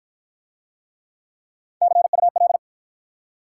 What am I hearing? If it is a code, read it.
Morse code, heard as a keyed tone, about 50 wpm, 700 Hz. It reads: CUL